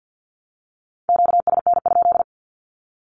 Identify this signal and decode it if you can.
Morse code, heard as a keyed tone, about 50 wpm, 700 Hz. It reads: QSN?
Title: Q-Codes Repeated 3x 50wpm